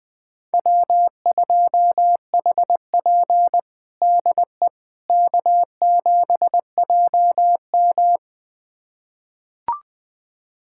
Morse code, heard as a keyed tone, about 20 wpm, 700 Hz. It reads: W2HP DE K7JM
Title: USA Callsign DE Callsign - 4 Characters - 20wpm